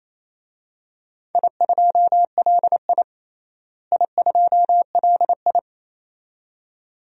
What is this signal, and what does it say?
Morse code, heard as a keyed tone, about 27 wpm, 700 Hz. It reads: I2LI I2LI
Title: DXCC-100 Call Signs - 4 Characters - Repeated 2x - 28wpm